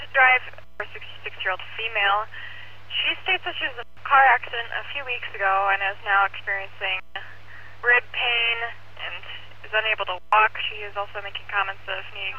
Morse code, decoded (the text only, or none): none